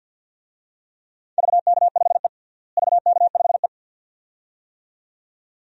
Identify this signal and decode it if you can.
Morse code, heard as a keyed tone, about 52 wpm, 700 Hz. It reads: VK5E VK5E
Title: DXCC-100 Call Signs - 4 Characters - Repeated 2x - 50wpm